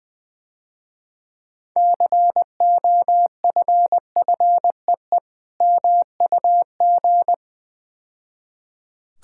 Morse code, heard as a keyed tone, about 21 wpm, 700 Hz. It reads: COFFEE MUG
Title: Easy Words - Speed-Racing with Hints - 20wpm